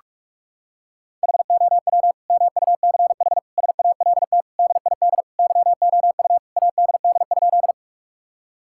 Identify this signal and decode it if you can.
Morse code, heard as a keyed tone, about 46 wpm, 700 Hz. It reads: HOW MUCH SALT DID YOU ADD?
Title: Sentences from Top 1000 Words - Review Part 5 - 45wpm